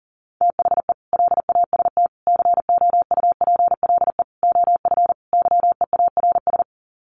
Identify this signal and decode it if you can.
Morse code, heard as a keyed tone, about 40 wpm, 700 Hz. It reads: THE LAST COUPLE OF YEARS
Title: Sets of 5 Words 40wpm